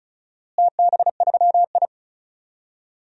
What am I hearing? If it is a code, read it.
Morse code, heard as a keyed tone, about 36 wpm, 700 Hz. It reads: TB3I
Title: DXCC-100 Call Signs - 4 Characters - Repeated 3x - 35wpm